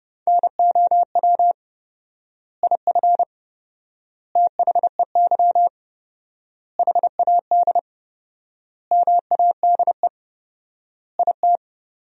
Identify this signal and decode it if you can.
Morse code, heard as a keyed tone, about 30 wpm, 700 Hz. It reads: NOW IF THEY HAD MADE IT
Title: 4x Word-Spacing - Sentences from Top 500 Words - 30wpm